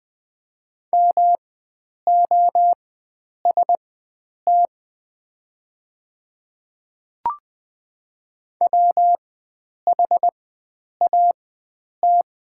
Morse code, heard as a keyed tone, about 20 wpm, spaced as for 9 wpm, 700 Hz. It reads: MOST WHAT
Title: 4x Character Spacing - Top 100 Words - 20wpm